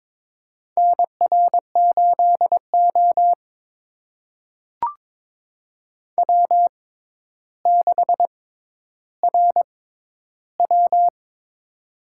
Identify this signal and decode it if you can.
Morse code, heard as a keyed tone, about 22 wpm, spaced as for 11 wpm, 700 Hz. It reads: NR8O W6RW
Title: USA Call Signs - 6x Character Spacing - 22wpm